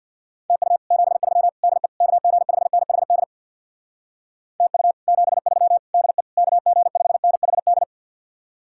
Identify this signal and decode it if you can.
Morse code, heard as a keyed tone, about 62 wpm, 700 Hz. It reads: TU 73 DE KG5NHD TU 73 DE KG5NHD
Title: Speed-Racing Common QSO Phrases - 50wpm